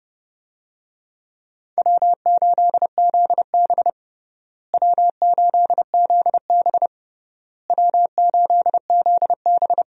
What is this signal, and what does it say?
Morse code, heard as a keyed tone, about 30 wpm, 700 Hz. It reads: W8ZB W8ZB W8ZB
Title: USA Call Signs - 4 Characters - Repeated 3x - 30wpm